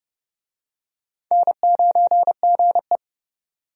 Morse code, heard as a keyed tone, about 31 wpm, 700 Hz. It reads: N9GE
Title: USA Call Signs - 4 Characters - 30wpm